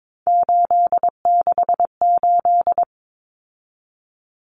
Morse code, heard as a keyed tone, about 22 wpm, 700 Hz. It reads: 868
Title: Numbers - 3-Digit 22wpm